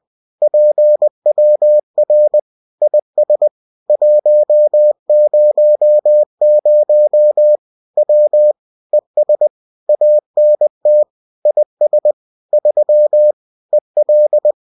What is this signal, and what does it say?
Morse code, heard as a keyed tone, about 20 wpm, 600 Hz. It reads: PWR IS 100 W ES ANT IS 3 EL